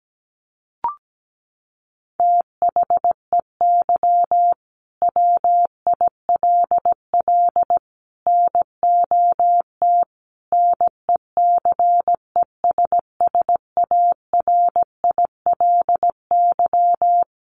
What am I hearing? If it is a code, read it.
Morse code, heard as a keyed tone, about 17 wpm, 700 Hz. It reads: THEY WILL NOT NECESSARILY
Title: Sentences from Top 1400 Words 17wpm